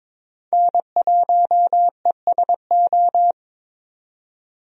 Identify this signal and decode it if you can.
Morse code, heard as a keyed tone, about 22 wpm, 700 Hz. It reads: N1ESO
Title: USA Call Signs - 5 Characters - 22wpm